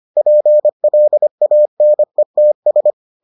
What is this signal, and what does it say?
Morse code, heard as a keyed tone, about 25 wpm, 600 Hz. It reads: PLANETS